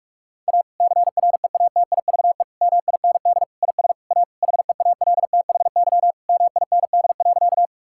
Morse code, heard as a keyed tone, about 41 wpm, 700 Hz. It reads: A CREATIVE MIND IS A HEALTHY MIND.
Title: Sentences from Top 1000 Words - Review Part 5 - 45wpm